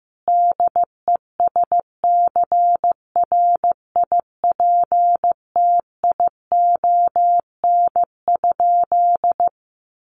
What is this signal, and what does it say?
Morse code, heard as a keyed tone, about 15 wpm, 700 Hz. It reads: DESCRIPTION?
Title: Sentences from Top 1300 Words 15wpm